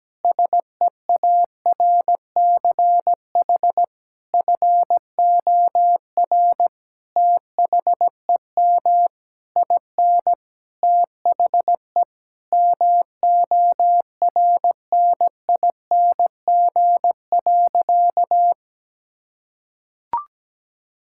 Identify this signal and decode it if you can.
Morse code, heard as a keyed tone, about 17 wpm, 700 Hz. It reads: SEARCH FOR THEM IN THE MORNING.
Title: Sentences from Top 1000 Words - Review Part 5 - 17wpm